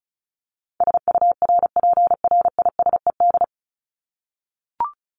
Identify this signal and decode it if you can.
Morse code, heard as a keyed tone, about 34 wpm, 700 Hz. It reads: SURPRISED
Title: Top 1501-1750 Words 35wpm